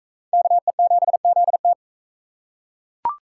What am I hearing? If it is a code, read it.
Morse code, heard as a keyed tone, about 36 wpm, 700 Hz. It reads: KE7ZT